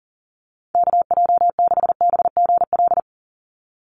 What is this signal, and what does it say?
Morse code, heard as a keyed tone, about 41 wpm, 700 Hz. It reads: KJ6BGL
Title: USA Call Signs 40wpm